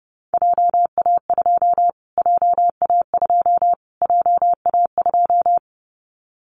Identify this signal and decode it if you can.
Morse code, heard as a keyed tone, about 30 wpm, 700 Hz. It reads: JA2 JA2 JA2
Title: Callsign Prefixes for Top 25 DXCC - Repeated 3x - 30wpm